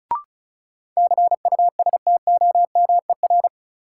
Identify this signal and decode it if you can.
Morse code, heard as a keyed tone, about 34 wpm, 700 Hz. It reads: CUSTOMER
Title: Sentences from Top 1100 Words - New Word Drill - 35wpm